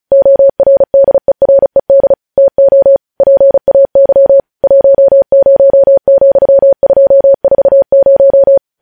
Morse code, heard as a keyed tone, about 36 wpm, 550 Hz. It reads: ORDERED TO PAY 10,240
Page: news headlines in morse code at 35 wpm